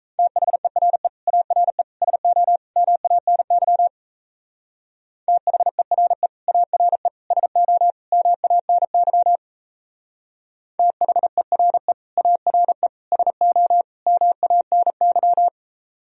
Morse code, heard as a keyed tone, about 32 wpm, 700 Hz. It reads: THERE ARE SO MANY THERE ARE SO MANY THERE ARE SO MANY
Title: Speed-Racing - Sets of 4 Words 28wpm